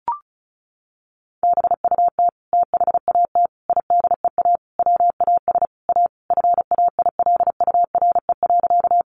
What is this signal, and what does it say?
Morse code, heard as a keyed tone, about 34 wpm, 700 Hz. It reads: BUT THAT IDEA WAS A FAILURE.
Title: Sentences from Top 900 Words 35wpm